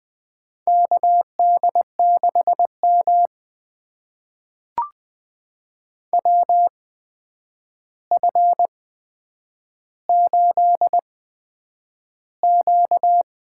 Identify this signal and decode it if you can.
Morse code, heard as a keyed tone, about 20 wpm, spaced as for 9 wpm, 700 Hz. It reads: KD6M WF8Q
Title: USA Call Signs - 4 Characters - 8x Character Spacing - 20wpm